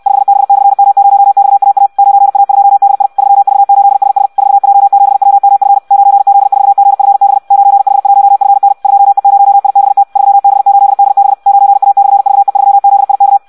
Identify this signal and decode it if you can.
Morse code, heard as a keyed tone, about 64 wpm, 800 Hz. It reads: 5B/M0XAA 9A2DI 3V8SS 4J7WMF 9Z4GFB 9H1FN 2E0IBT 3B8DB 9A/VE3ZIK
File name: Audio practica CW / qrq-tx-66 2